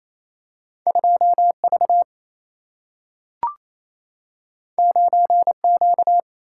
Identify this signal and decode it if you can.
Morse code, heard as a keyed tone, about 28 wpm, 700 Hz. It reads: 2 V 9 Q
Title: Single-Letter Single-Number 28wpm